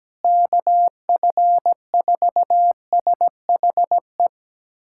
Morse code, heard as a keyed tone, about 17 wpm, 700 Hz. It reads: KF4SHE
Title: USA Callsign DE Callsign 17wpm